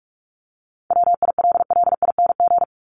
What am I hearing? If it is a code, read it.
Morse code, heard as a keyed tone, about 44 wpm, 700 Hz. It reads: WILLING
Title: Top 751-1000 Words 45wpm